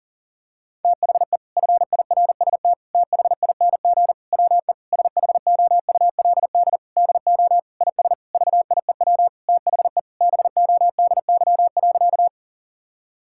Morse code, heard as a keyed tone, about 40 wpm, 700 Hz. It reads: THE FIRST THING WE SHOULD DO IS VIEW THE BODY.
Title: Sentences from Top 1000 Words - Review Part 6 - 40wpm